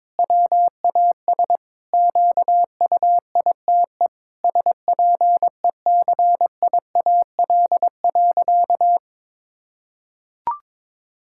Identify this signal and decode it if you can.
Morse code, heard as a keyed tone, about 22 wpm, 700 Hz. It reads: WAS QUITE SPECIAL.
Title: Sentences from Top 1600 Words 22wpm